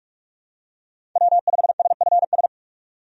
Morse code, heard as a keyed tone, about 46 wpm, 700 Hz. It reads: W5SFS
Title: USA Call Signs - 5 Characters - 45wpm